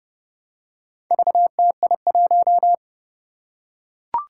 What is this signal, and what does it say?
Morse code, heard as a keyed tone, about 30 wpm, 700 Hz. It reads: VTI1
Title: Four Challenging Letter-Number Combinations 30wpm